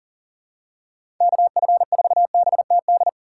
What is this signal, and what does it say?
Morse code, heard as a keyed tone, about 40 wpm, 700 Hz. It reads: KF4BTD
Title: USA Call Signs - 3x Character Spacing - 40wpm